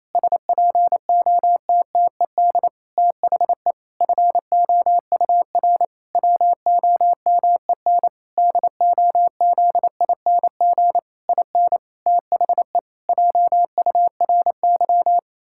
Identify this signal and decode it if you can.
Morse code, heard as a keyed tone, about 28 wpm, 700 Hz. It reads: SPOTTED THE FOUR WOMEN DOZING IN THE JURY
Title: Pangram Challenge v2 - 28wpm